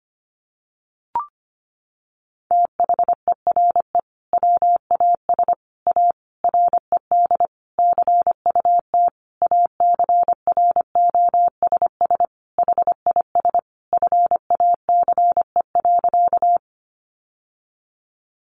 Morse code, heard as a keyed tone, about 25 wpm, 700 Hz. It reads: THERE WAS A RED CUT ACROSS HIS FACE.
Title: Sentences from Top 700 Words 25wpm